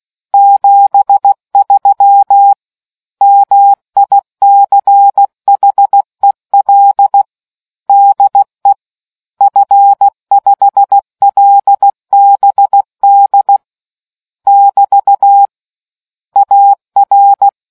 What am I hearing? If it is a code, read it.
Morse code, heard as a keyed tone, about 16 wpm, 800 Hz. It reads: 73 MICHEL DE F5LBD <BT> AR